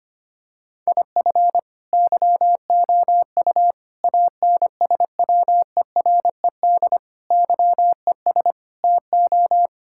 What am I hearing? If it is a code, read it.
Morse code, heard as a keyed tone, about 25 wpm, 700 Hz. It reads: IF YOU ANSWERED YES TO
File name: practice / 25wpm / Sentences with 9-Word Phrases - 25wpm